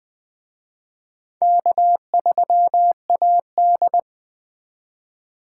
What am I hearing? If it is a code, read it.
Morse code, heard as a keyed tone, about 23 wpm, 700 Hz. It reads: K3AD
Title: Speed Racing - USA 1x2 Call Signs 20wpm